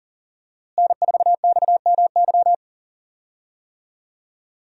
Morse code, heard as a keyed tone, about 40 wpm, 700 Hz. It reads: N4XMY
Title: USA Call Signs - 5 Characters - Repeated 3x - 40wpm